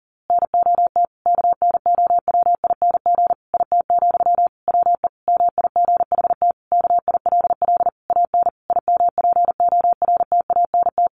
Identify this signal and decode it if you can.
Morse code, heard as a keyed tone, about 40 wpm, 700 Hz. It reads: NOT KNOWING IT, WE MIGHT KILL AN IMPORTANT
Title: A Sound Of Thunder 40wpm